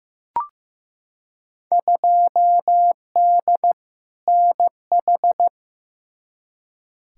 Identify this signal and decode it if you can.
Morse code, heard as a keyed tone, about 15 wpm, 700 Hz. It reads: 2D NH
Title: Field Day Exchanges Encore - 15wpm